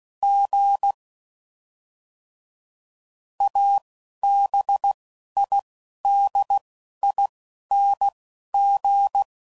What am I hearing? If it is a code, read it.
Morse code, heard as a keyed tone, about 16 wpm, spaced as for 12 wpm, 800 Hz. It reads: G ABIDING
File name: Suffixes / Special INGs / Single INGs